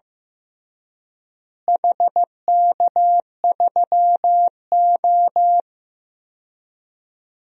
Morse code, heard as a keyed tone, about 15 wpm, 700 Hz. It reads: HK3O
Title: DXCC-100 Call Signs - 4 Characters - Repeated 2x - 15wpm